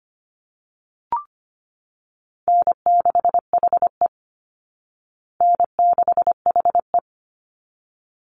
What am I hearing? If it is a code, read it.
Morse code, heard as a keyed tone, about 25 wpm, 700 Hz. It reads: N6HE N6HE
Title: Call Signs Repeated 3x - 25wpm